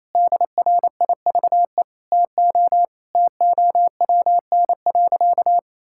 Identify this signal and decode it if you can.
Morse code, heard as a keyed tone, about 28 wpm, 700 Hz. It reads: DRIVE TO TOWN.
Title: Sentences from Top 1000 Words - Review Part 6 - 28wpm